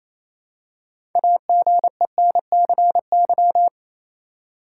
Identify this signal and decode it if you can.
Morse code, heard as a keyed tone, about 28 wpm, 700 Hz. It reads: AGENCY
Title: Six Letter Words - 28wpm